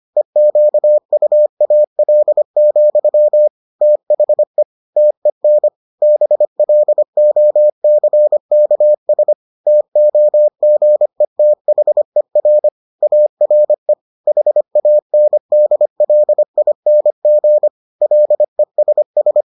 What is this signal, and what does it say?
Morse code, heard as a keyed tone, about 25 wpm, 600 Hz. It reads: EQUAL, THE TEN BLOCKS TOGETHER ARE HANDLING LESS